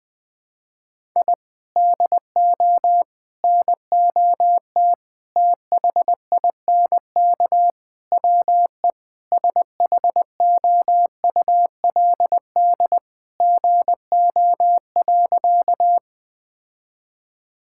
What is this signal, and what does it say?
Morse code, heard as a keyed tone, about 20 wpm, 700 Hz. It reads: I DO NOT THINK WE SHOULD GO.
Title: Sentences with 4-Word Phrases - Spoken Beginning - 20wpm